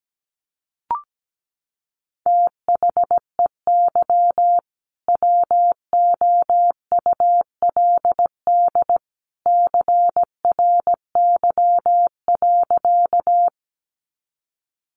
Morse code, heard as a keyed tone, about 17 wpm, 700 Hz. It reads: THEY WOULD CRY.
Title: Sentences from Top 500 Words 17wpm